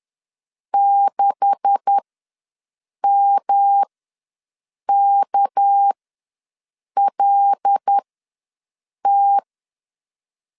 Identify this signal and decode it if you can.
Morse code, heard as a keyed tone, about 10 wpm, spaced as for 6 wpm, 800 Hz. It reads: 6MKLT
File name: LEZIONI / 9 286 ws 10wpm 4032